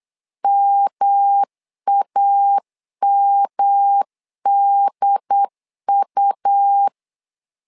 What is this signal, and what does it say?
Morse code, heard as a keyed tone, about 8 wpm, 800 Hz. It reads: MAMDU
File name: LEZIONI / 3 MWRO / 3 M 8wpm 2300